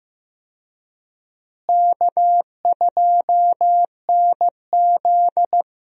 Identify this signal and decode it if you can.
Morse code, heard as a keyed tone, about 15 wpm, 700 Hz. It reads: K2NZ
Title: USA Call Signs - 4 Characters - Repeated 3x - 15wpm